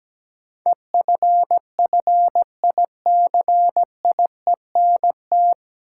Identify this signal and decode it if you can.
Morse code, heard as a keyed tone, about 17 wpm, 700 Hz. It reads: EFFICIENT
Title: Sentences from Top 1300 Words - New Word Drill Alt - 17wpm